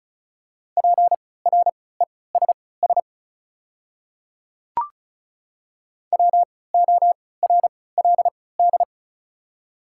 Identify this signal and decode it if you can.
Morse code, heard as a keyed tone, about 36 wpm, spaced as for 20 wpm, 690 Hz. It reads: PRESS WORLD
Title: 3x Character Spacing - Top 500 Words - 35wpm